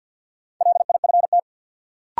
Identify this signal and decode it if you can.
Morse code, heard as a keyed tone, about 52 wpm, 700 Hz. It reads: LIFT
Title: Sentences from Top 1000 Words - New Word Drill - 50wpm